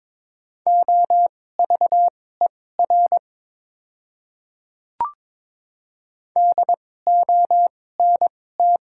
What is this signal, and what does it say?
Morse code, heard as a keyed tone, about 22 wpm, spaced as for 16 wpm, 700 Hz. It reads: OVER DONT
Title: 2x Character Spacing - Top 300 Words - 22wpm